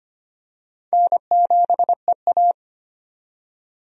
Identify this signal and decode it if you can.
Morse code, heard as a keyed tone, about 25 wpm, 700 Hz. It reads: N7EA
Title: USA Call Signs - 4 Characters - 25wpm